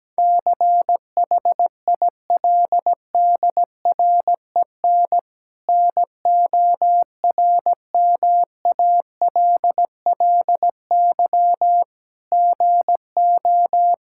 Morse code, heard as a keyed tone, about 17 wpm, 700 Hz. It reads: CHILDREN NORMALLY GO
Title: Sentences from Top 1000 Words - Review Part 6 - 17wpm